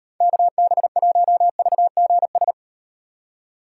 Morse code, heard as a keyed tone, about 38 wpm, 700 Hz. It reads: KB1VGS